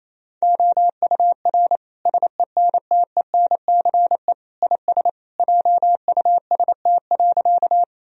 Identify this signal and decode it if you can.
Morse code, heard as a keyed tone, about 28 wpm, 700 Hz. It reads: OUR SENTENCE IS JUST.
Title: Sentences from Top 1000 Words - Review Part 1 - 28wpm